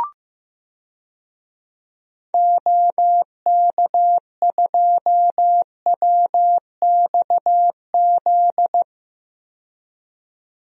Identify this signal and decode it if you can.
Morse code, heard as a keyed tone, about 15 wpm, 700 Hz. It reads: OK2WXZ
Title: DXCC-100 Call Signs - 6 Characters - 15wpm